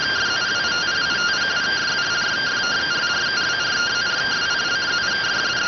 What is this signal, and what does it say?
Two-tone teleprinter signal, about 170 Hz shift.